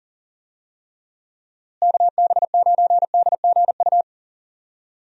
Morse code, heard as a keyed tone, about 40 wpm, 700 Hz. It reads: KB9DGU
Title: USA Call Signs - 6 Characters - Repeated 3x - 40wpm